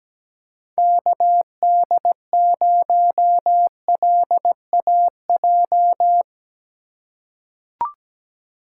Morse code, heard as a keyed tone, about 17 wpm, 700 Hz. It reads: KD0LAJ
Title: USA Call Signs - 3x Character Spacing - 17wpm